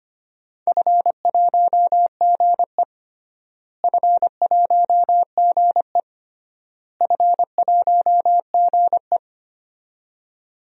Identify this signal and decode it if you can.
Morse code, heard as a keyed tone, about 25 wpm, 700 Hz. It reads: F1GE F1GE F1GE
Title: DXCC-100 Call Signs - 4 Characters - Repeated 3x - 25wpm